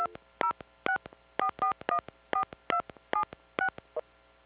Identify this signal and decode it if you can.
DTMF, heard as telephone touch tones, tones of about 90 ms. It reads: 2064414276